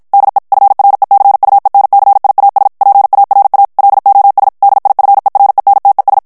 Morse code, heard as a keyed tone, about 51 wpm, 800 Hz. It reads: DE FRECUENCIAS PARA LOS DIFERENTES
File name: ESPAGNOL TEXTE EN CLAIR 2 VITESSE 50 MOTS MINUTES